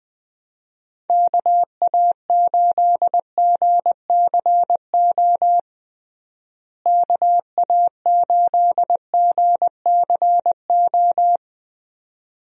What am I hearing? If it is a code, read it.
Morse code, heard as a keyed tone, about 21 wpm, 700 Hz. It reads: KA8GCO KA8GCO
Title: USA Call Signs - 6 Characters - Repeated 2x - 20wpm